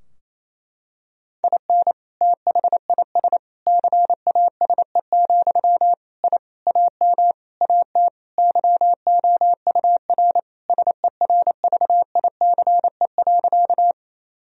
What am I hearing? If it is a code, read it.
Morse code, heard as a keyed tone, about 28 wpm, 700 Hz. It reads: IN THIS CASE, I AM AT YOUR SERVICE.
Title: Sentences from Top 1000 Words - Review Part 4 - 28wpm